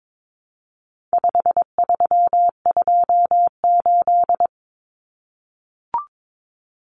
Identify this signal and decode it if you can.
Morse code, heard as a keyed tone, about 22 wpm, 700 Hz. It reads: 5328
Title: Numbers - 4-Digit 22wpm